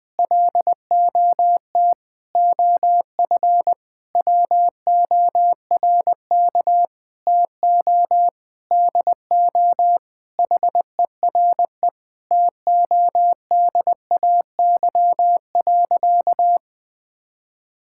Morse code, heard as a keyed tone, about 20 wpm, 700 Hz. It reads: LOT OF WORK TO DO HERE TODAY.
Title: Sentences with 7-Word Phrases - 20wpm